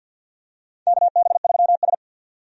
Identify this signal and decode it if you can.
Morse code, heard as a keyed tone, about 53 wpm, 700 Hz. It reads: KB3S
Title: USA Call Signs - 4 Characters - 50wpm